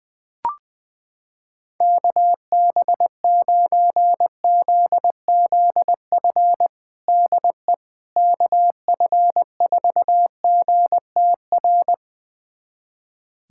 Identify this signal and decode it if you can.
Morse code, heard as a keyed tone, about 20 wpm, 700 Hz. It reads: KB9ZZF DE KF4GTR
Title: USA Callsign DE Callsign - 6 Characters - 20wpm